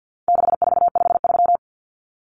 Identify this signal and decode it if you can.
Morse code, heard as a keyed tone, about 50 wpm, 700 Hz. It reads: 6453
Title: Numbers - 4-Digit 50wpm